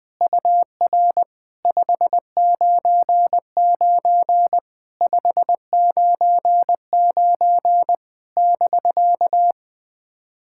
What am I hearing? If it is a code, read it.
Morse code, heard as a keyed tone, about 20 wpm, 700 Hz. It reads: UR 599 599 <BK>
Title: POTA QSO Practice - 20wpm